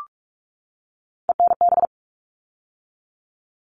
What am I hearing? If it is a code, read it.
Morse code, heard as a keyed tone, about 44 wpm, 700 Hz. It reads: ENB
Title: All Cut Numbers - 3-Digit 45wpm